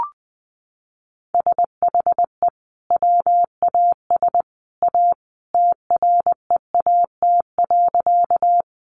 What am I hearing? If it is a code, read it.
Morse code, heard as a keyed tone, about 20 wpm, 700 Hz. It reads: SHE WAS A TREAT.
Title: Sentences from Top 1300 Words 20wpm